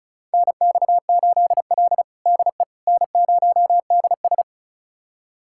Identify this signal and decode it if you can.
Morse code, heard as a keyed tone, about 36 wpm, 700 Hz. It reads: NX8L DE N0DS
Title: USA Callsign DE Callsign - 4 Characters - 35wpm